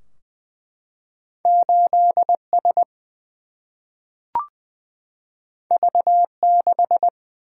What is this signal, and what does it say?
Morse code, heard as a keyed tone, about 20 wpm, 700 Hz. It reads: 8S V6
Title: Single-Letter Single-Number 20wpm